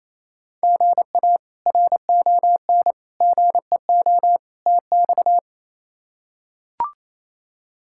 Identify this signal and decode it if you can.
Morse code, heard as a keyed tone, about 28 wpm, 700 Hz. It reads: GA RON GEO TX
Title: K1USN SST QSO Practice - 28wpm